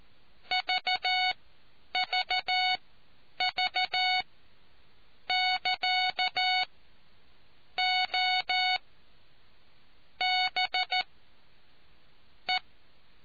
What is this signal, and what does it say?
Morse code, heard as a keyed tone, about 14 wpm, spaced as for 6 wpm, 740 Hz. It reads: VVV <KA> OBE